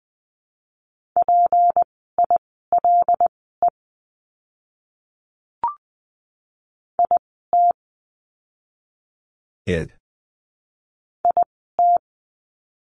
Morse code, heard as a keyed tone, about 19 wpm, spaced as for 14 wpm, 700 Hz. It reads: PILE IT IT
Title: 153 Lesson 25f - Letter p - Words 20wpm